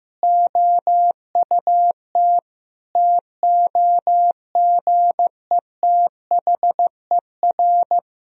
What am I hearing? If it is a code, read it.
Morse code, heard as a keyed tone, about 15 wpm, 700 Hz. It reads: OUT TOGETHER
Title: Sentences from Top 1000 Words - Review Part 3 - 15wpm